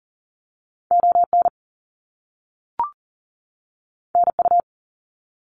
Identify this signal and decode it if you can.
Morse code, heard as a keyed tone, about 40 wpm, 700 Hz. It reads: ON NU
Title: Canadian Provinces and Territories 40wpm